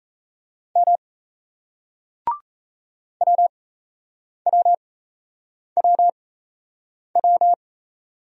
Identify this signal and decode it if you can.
Morse code, heard as a keyed tone, about 35 wpm, spaced as for 8 wpm, 700 Hz. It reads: M WWWW
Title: Speed-Racing Single Letters 28wpm